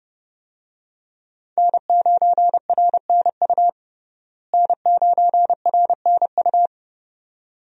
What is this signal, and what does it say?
Morse code, heard as a keyed tone, about 30 wpm, 700 Hz. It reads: N9RNU N9RNU
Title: USA Call Signs - 5 Characters - Repeated 2x - 30wpm